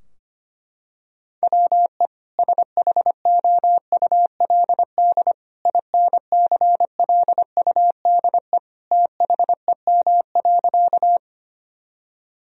Morse code, heard as a keyed tone, about 25 wpm, 700 Hz. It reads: WE SHOULD INCLUDE THEM.
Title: Sentences from Top 1000 Words - Review Part 3 - 25wpm